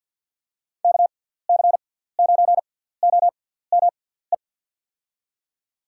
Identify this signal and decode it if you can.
Morse code, heard as a keyed tone, about 48 wpm, spaced as for 17 wpm, 700 Hz. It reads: KC9OME
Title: USA Call Signs - 6 Characters - 6x Character Spacing - 50wpm